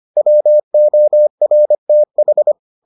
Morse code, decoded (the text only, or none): WORTH